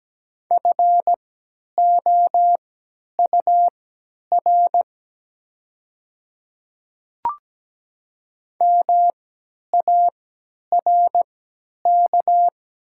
Morse code, heard as a keyed tone, about 17 wpm, spaced as for 10 wpm, 700 Hz. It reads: FOUR MARK
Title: 3x Character Spacing - Top 300 Words - 17wpm